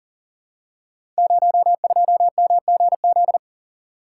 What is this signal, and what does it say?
Morse code, heard as a keyed tone, about 42 wpm, 700 Hz. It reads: 02MGZ
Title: Code Groups - 5 Letter:Numbers - 40wpm